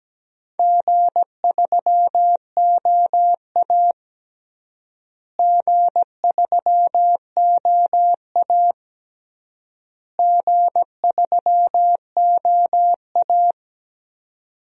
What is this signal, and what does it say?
Morse code, heard as a keyed tone, about 17 wpm, 700 Hz. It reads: G3OA G3OA G3OA
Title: DXCC-100 Call Signs - 4 Characters - Repeated 3x - 17wpm